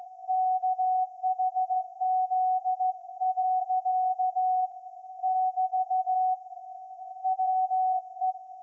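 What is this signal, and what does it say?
Morse code, heard as a keyed tone, about 16 wpm, 740 Hz. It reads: KHZ. <BT> WE
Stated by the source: Morse code from SAQ, received at a station in Putignano, Apulia, Italy, on VLF.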